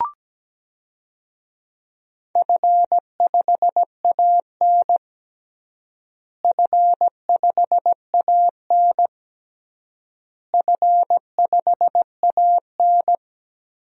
Morse code, heard as a keyed tone, about 17 wpm, 700 Hz. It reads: F5AN F5AN F5AN
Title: DXCC-100 Call Signs - 4 Characters - Repeated 3x - 17wpm